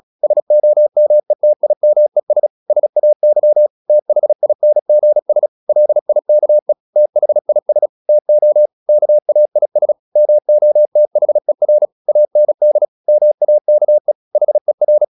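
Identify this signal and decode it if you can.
Morse code, heard as a keyed tone, about 36 wpm, 600 Hz. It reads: SOMETIMES SAY THINGS LIKE THIS TO KAIS MOTHER AND MAKE HER
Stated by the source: Morse practice at 36 wpm, text of Ken Follett's novel Never, from CW med Gnister.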